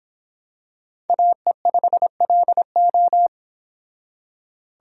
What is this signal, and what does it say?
Morse code, heard as a keyed tone, about 26 wpm, 700 Hz. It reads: AE5LO